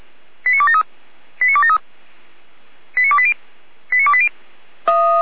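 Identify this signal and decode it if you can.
Five-tone selective call (ZVEI), heard as quick runs of five tones, tones of about 70 ms. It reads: 89273 89273 89280 89280
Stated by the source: Received by SDR, ZVEI five-tone selective call.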